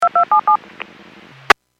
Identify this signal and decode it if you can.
DTMF, heard as telephone touch tones, tones of about 110 ms. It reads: 2277